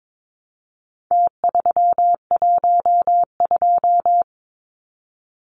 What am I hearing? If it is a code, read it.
Morse code, heard as a keyed tone, about 22 wpm, 700 Hz. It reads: T312
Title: Numbers - Contest Style - 4-Digit 22wpm